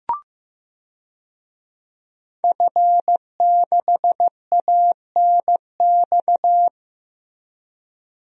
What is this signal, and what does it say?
Morse code, heard as a keyed tone, about 15 wpm, 700 Hz. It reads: F6ANX
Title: DXCC-100 Call Signs - 5 Characters - 15wpm